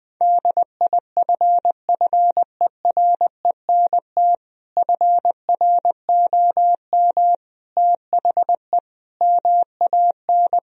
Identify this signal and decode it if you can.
Morse code, heard as a keyed tone, about 20 wpm, 700 Hz. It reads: DIFFERENT FROM THE MAN
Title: Sentences from Top 1000 Words - Review Part 3 - 20wpm